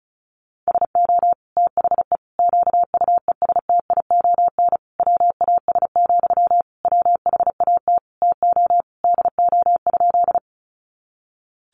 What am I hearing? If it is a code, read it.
Morse code, heard as a keyed tone, about 34 wpm, 700 Hz. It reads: SO THE QUESTION WAS, WHAT TO DO?
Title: Sentences from Top 400 Words 35wpm